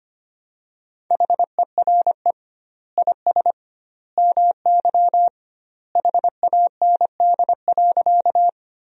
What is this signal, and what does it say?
Morse code, heard as a keyed tone, about 25 wpm, 700 Hz. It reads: HERE IS MY HAND.
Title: 2x Word-Spacing - Sentences from Top 200 Words - 25wpm